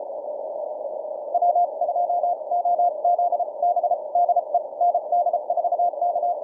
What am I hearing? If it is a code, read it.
Morse code, heard as a keyed tone, about 33 wpm, spaced as for 28 wpm, 700 Hz. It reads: W2OZB DE ND4K
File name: archived / CWBros